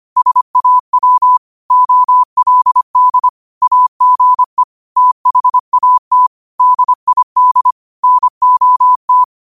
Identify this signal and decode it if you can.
Morse code, heard as a keyed tone, about 25 wpm, 1000 Hz. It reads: SAW OLD AGE THAT DID NOT
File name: qotdfiles / 25wpm